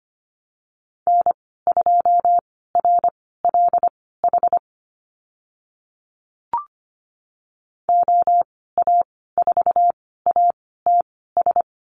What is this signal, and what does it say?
Morse code, heard as a keyed tone, about 25 wpm, spaced as for 16 wpm, 700 Hz. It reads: N2RLH OA4ATS
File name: Lesson 15g - Number 5 - Call Signs 25wpm